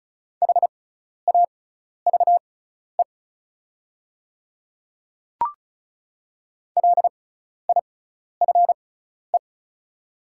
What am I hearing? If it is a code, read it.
Morse code, heard as a keyed tone, about 34 wpm, spaced as for 12 wpm, 700 Hz. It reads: HAVE LIFE
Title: 6x Character Spacing - Top 500 Words - 35wpm